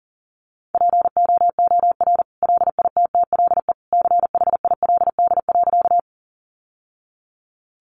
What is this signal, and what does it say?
Morse code, heard as a keyed tone, about 37 wpm, 700 Hz. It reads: POOR LITTLE CHILD.
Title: Sentences from Top 1000 Words 40wpm